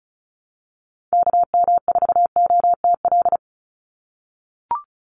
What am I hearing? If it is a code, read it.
Morse code, heard as a keyed tone, about 34 wpm, 700 Hz. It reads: KM4OTL
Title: USA Call Signs 35wpm